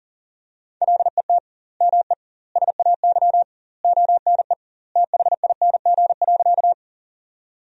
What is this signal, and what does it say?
Morse code, heard as a keyed tone, about 37 wpm, 700 Hz. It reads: LET ME SAY ONE THING.
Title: 2x Word-Spacing - Sentences from Top 300 Words - 40wpm